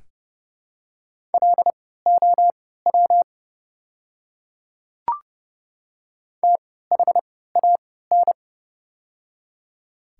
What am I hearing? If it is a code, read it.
Morse code, heard as a keyed tone, about 29 wpm, spaced as for 17 wpm, 700 Hz. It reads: LOW THAN
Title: 3x Character Spacing - Top 300 Words - 30wpm